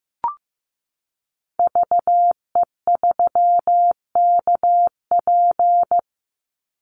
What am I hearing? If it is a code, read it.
Morse code, heard as a keyed tone, about 15 wpm, 700 Hz. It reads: VE3KP
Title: Call Signs Repeated 3x - 15wpm